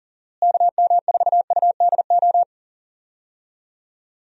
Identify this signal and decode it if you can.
Morse code, heard as a keyed tone, about 40 wpm, 700 Hz. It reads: KM4UDO
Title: USA Call Signs - 6 Characters - Repeated 2x - 40wpm